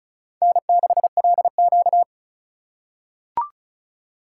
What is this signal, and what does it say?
Morse code, heard as a keyed tone, about 36 wpm, 700 Hz. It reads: N6LQ
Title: USA Call Signs - 4 Characters - 3x Character Spacing - 35wpm